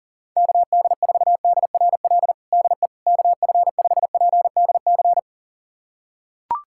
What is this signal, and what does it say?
Morse code, heard as a keyed tone, about 40 wpm, 700 Hz. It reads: KD4DRL DE KF5PDC
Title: USA Callsign DE Callsign - 6 Characters - 40wpm